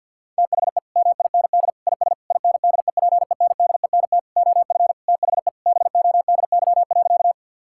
Morse code, heard as a keyed tone, about 52 wpm, 700 Hz. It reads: THE MIND IS INDEPENDENT OF THE BODY.
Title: Sentences from Top 1000 Words - Review Part 6 - 50wpm